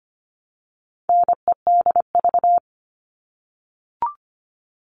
Morse code, heard as a keyed tone, about 25 wpm, 700 Hz. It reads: NEDV
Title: All Cut Numbers - 4-Digit 25wpm